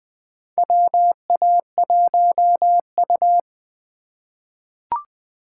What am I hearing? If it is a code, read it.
Morse code, heard as a keyed tone, about 20 wpm, 700 Hz. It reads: WA1U
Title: USA Call Signs - 4 Characters - 4x Character Spacing - 20wpm